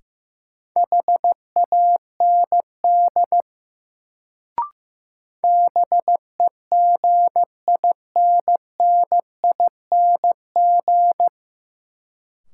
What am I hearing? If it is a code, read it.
Morse code, heard as a keyed tone, about 15 wpm, 700 Hz. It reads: HAND BEGINNING
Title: Sets of 3 Words - New Word Drill - 15wpm